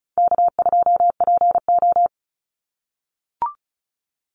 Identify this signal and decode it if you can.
Morse code, heard as a keyed tone, about 36 wpm, 700 Hz. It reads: K2PO
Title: Call Signs - 35wpm